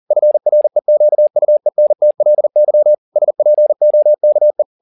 Morse code, heard as a keyed tone, about 40 wpm, 600 Hz. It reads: FREQUENTLY SPOKE